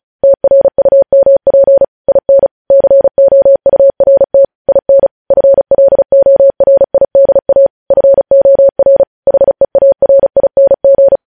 Morse code, heard as a keyed tone, about 36 wpm, 550 Hz. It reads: TRUMP IN COURT IN FLORIDA FOR HEARING
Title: News Headlines at 35